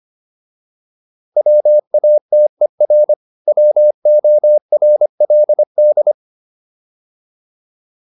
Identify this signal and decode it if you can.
Morse code, heard as a keyed tone, about 25 wpm, 600 Hz. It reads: WATER WORLD
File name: PR405 25